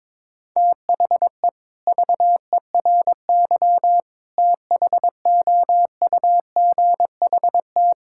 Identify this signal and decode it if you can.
Morse code, heard as a keyed tone, about 22 wpm, 700 Hz. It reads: THE VERY THOUGHT
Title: Sentences from Top 1000 Words - Review Part 5 - 22wpm